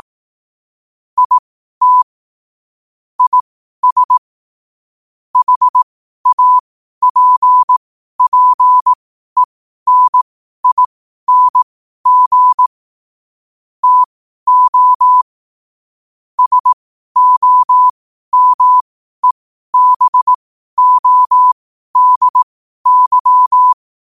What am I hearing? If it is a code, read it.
Morse code, heard as a keyed tone, about 18 wpm, spaced as for 12 wpm, 1000 Hz. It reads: IT IS HAPPENING TO SOMEBODY
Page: New quotes every day in morse code at 12 Words per minute.